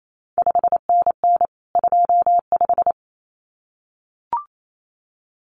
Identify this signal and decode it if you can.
Morse code, heard as a keyed tone, about 28 wpm, 700 Hz. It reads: 5NN 25
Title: CQ WW DX Contest Exchange - 28wpm